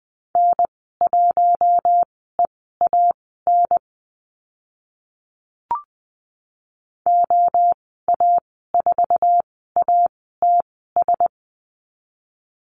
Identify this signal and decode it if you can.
Morse code, heard as a keyed tone, about 20 wpm, spaced as for 14 wpm, 700 Hz. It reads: N1EAN OA4ATS
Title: 055 Lesson 10g - Letter r - Call Signs 20wpm